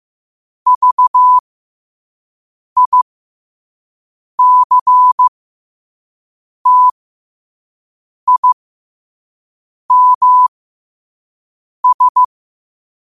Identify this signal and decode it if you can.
Morse code, heard as a keyed tone, about 15 wpm, spaced as for 5 wpm, 1000 Hz. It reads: VICTIMS
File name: qotdfiles / 5wpm